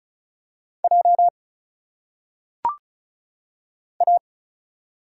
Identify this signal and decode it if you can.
Morse code, heard as a keyed tone, about 36 wpm, 700 Hz. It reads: J A
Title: Single Letters - Phonetic Pronunciation 35wpm